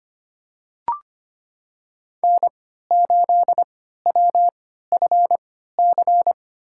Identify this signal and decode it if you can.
Morse code, heard as a keyed tone, about 25 wpm, spaced as for 14 wpm, 700 Hz. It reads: N8WFC
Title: USA Call Signs - 5 Characters - 3x Character Spacing - 25wpm